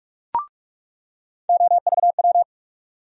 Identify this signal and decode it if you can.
Morse code, heard as a keyed tone, about 43 wpm, 700 Hz. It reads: OVW
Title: USA Call Sign Postfixes - 3 Letters 45wpm